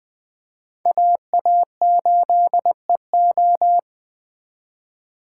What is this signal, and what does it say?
Morse code, heard as a keyed tone, about 25 wpm, 700 Hz. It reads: AA8EO
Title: Speed Racing - USA 2x2 Call Signs 20wpm